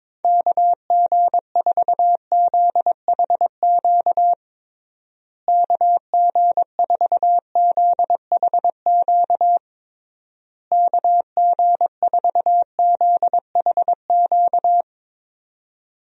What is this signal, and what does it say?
Morse code, heard as a keyed tone, about 21 wpm, 700 Hz. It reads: KG4ZHQ KG4ZHQ KG4ZHQ
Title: USA Call Signs - 6 Characters - Repeated 3x - 22wpm